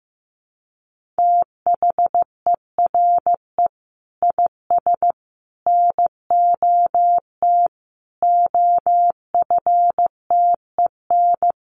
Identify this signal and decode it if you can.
Morse code, heard as a keyed tone, about 14 wpm, 700 Hz. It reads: THERE IS NOT OFTEN
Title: Sentences from Top 1400 Words 15wpm